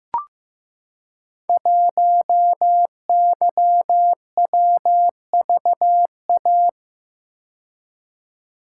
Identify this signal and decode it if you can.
Morse code, heard as a keyed tone, about 15 wpm, 700 Hz. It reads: 1YWVA
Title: Code Groups - 5 Letter:Numbers - 15wpm